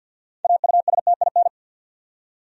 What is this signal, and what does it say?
Morse code, heard as a keyed tone, about 50 wpm, 700 Hz. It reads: AUSTIN
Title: USA Common Names - 50wpm